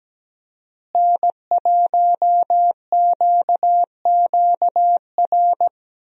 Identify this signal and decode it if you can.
Morse code, heard as a keyed tone, about 17 wpm, 700 Hz. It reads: N1QQR
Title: USA Call Signs - 5 Characters - Repeated 2x - 17wpm